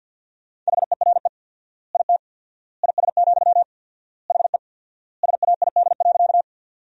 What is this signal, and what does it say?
Morse code, heard as a keyed tone, about 48 wpm, 700 Hz. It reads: HERE IT IS, HE SAID.
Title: 4x Word-Spacing - Sentences from Top 200 Words - 50wpm